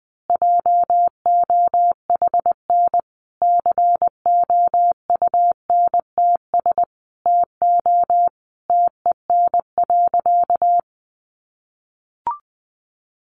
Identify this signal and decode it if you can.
Morse code, heard as a keyed tone, about 20 wpm, 700 Hz. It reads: JOHN COUNTS TO TEN.
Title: Easy Phrases 20wpm